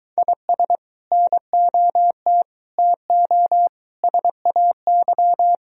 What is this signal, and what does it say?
Morse code, heard as a keyed tone, about 24 wpm, 700 Hz. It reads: IS NOT TO SAY